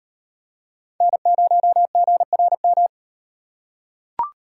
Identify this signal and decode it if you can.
Morse code, heard as a keyed tone, about 38 wpm, 700 Hz. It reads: N0GRM